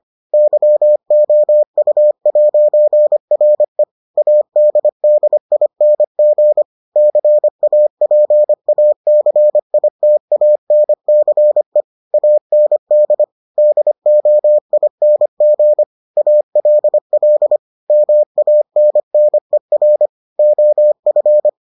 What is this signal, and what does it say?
Morse code, heard as a keyed tone, about 25 wpm, 600 Hz. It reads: YOU'RE ADDING CAPACITANCE AND DOING ALL MANNER OF